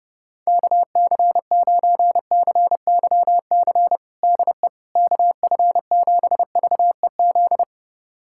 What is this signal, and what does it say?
Morse code, heard as a keyed tone, about 32 wpm, 700 Hz. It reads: KC9CYC DE KF7VEZ
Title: USA Callsign DE Callsign 30wpm